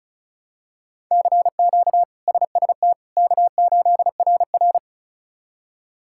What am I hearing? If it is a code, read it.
Morse code, heard as a keyed tone, about 34 wpm, 700 Hz. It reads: CQ SST K8RR
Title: K1USN SST QSO Practice - 35wpm